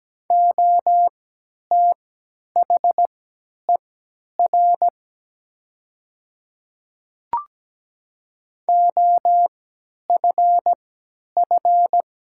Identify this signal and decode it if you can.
Morse code, heard as a keyed tone, about 17 wpm, spaced as for 10 wpm, 700 Hz. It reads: OTHER OFF
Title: 3x Character Spacing - Top 200 Words - 17wpm